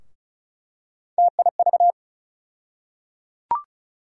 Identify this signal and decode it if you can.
Morse code, heard as a keyed tone, about 34 wpm, 700 Hz. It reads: TIV
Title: USA Call Sign Postfixes - 3 Letters 35wpm